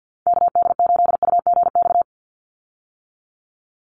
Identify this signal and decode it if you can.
Morse code, heard as a keyed tone, about 50 wpm, 700 Hz. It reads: KD8UGX
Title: USA Call Signs 50wpm